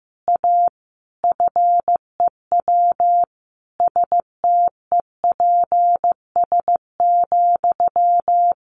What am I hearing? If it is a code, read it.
Morse code, heard as a keyed tone, about 15 wpm, 700 Hz. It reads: A FEW STEPS,
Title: A Sound Of Thunder 15wpm